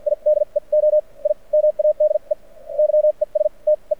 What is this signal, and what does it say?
Morse code, heard as a keyed tone, about 56 wpm, spaced as for 40 wpm, 600 Hz. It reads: IDEO I MADE YESTE
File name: archived / CRSnet